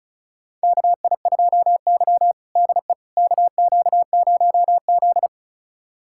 Morse code, heard as a keyed tone, about 34 wpm, 700 Hz. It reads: KI2Y DE KQ0Z
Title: USA Callsign DE Callsign - 4 Characters - 35wpm